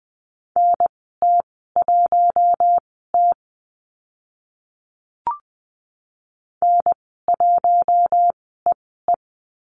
Morse code, heard as a keyed tone, about 20 wpm, spaced as for 14 wpm, 700 Hz. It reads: NT1T N1EE
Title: 042 Lesson 8g - Number 1 - Call Signs 20wpm